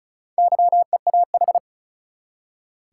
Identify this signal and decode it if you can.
Morse code, heard as a keyed tone, about 36 wpm, 700 Hz. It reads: YEAH
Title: Four Letter Words - 35wpm